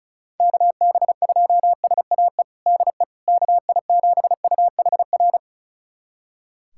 Morse code, heard as a keyed tone, about 34 wpm, 700 Hz. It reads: KB2SAE DE KI7UHR
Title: USA Callsign DE Callsign - 6 Characters - 35wpm